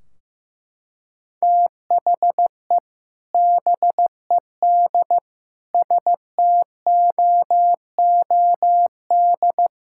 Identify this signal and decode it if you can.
Morse code, heard as a keyed tone, about 16 wpm, 700 Hz. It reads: THE BED STOOD
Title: Sentences from Top 1600 Words 15wpm